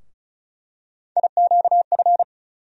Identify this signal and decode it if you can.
Morse code, heard as a keyed tone, about 36 wpm, 700 Hz. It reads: IQF
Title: USA Call Sign Postfixes - 3 Letters 35wpm